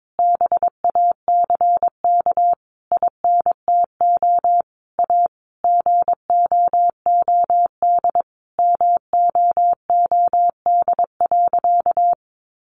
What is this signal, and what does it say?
Morse code, heard as a keyed tone, about 22 wpm, 700 Hz. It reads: BACK INTO A GOOD MOOD.
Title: Sentences from Top 1100 Words 22wpm